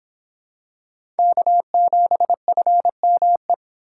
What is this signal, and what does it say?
Morse code, heard as a keyed tone, about 26 wpm, 700 Hz. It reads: K7FME